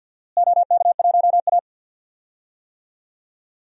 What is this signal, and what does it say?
Morse code, heard as a keyed tone, about 47 wpm, 700 Hz. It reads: OK1A
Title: DXCC-100 Call Signs - 4 Characters - Repeated 2x - 50wpm